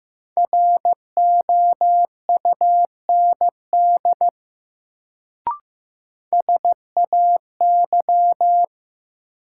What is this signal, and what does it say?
Morse code, heard as a keyed tone, about 15 wpm, 700 Hz. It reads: ROUND SAY
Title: Sentences from Top 200 Words - New Word Drill - 15wpm